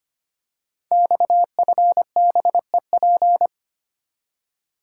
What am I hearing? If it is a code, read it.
Morse code, heard as a keyed tone, about 25 wpm, 700 Hz. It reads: XFBEP
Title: Code Groups - 5 Letter:Numbers - 25wpm